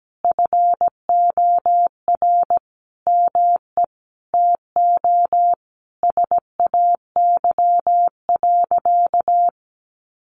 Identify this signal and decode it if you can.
Morse code, heard as a keyed tone, about 17 wpm, 700 Hz. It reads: FOR ME TO SAY.
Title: Sentences from Top 600 Words 17wpm